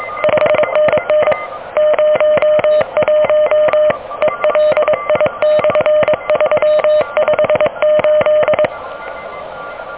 Morse code, heard as a keyed tone, about 23 wpm, 610 Hz. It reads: 5NN 01 ELI/358
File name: OK QRP 2023 / OK QRP 2023